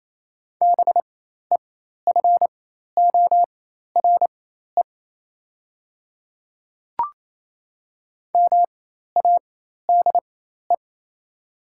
Morse code, heard as a keyed tone, about 28 wpm, spaced as for 13 wpm, 700 Hz. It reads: BEFORE MADE
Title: 4x Character Spacing - Top 300 Words - 28wpm